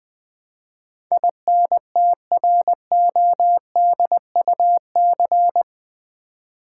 Morse code, heard as a keyed tone, about 20 wpm, 700 Hz. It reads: INTRODUC
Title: Build A Word - 4 Letters At A Time - 12-Letter Words - 20wpm